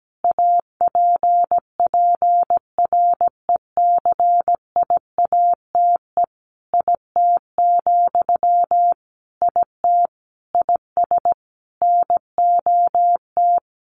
Morse code, heard as a keyed tone, about 17 wpm, 700 Hz. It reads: APPRECIATE IT, IT IS NOT
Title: Sentences from Top 1300 Words 17wpm